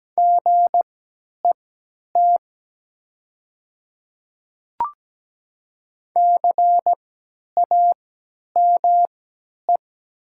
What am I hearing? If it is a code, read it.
Morse code, heard as a keyed tone, about 17 wpm, spaced as for 10 wpm, 700 Hz. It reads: GET CAME
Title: 3x Character Spacing - Top 200 Words - 17wpm